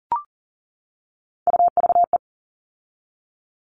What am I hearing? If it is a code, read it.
Morse code, heard as a keyed tone, about 40 wpm, 700 Hz. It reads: UVE
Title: All Cut Numbers - 3-Digit 40wpm